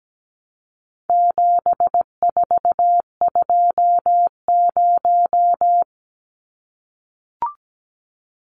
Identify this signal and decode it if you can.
Morse code, heard as a keyed tone, about 17 wpm, 700 Hz. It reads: 7420
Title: Numbers - 4-Digit 17wpm